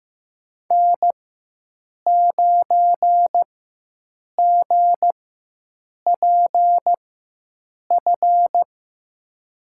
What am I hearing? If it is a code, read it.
Morse code, heard as a keyed tone, about 15 wpm, spaced as for 7 wpm, 700 Hz. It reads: N9GPF
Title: USA Call Signs - 4x Character Spacing - 15wpm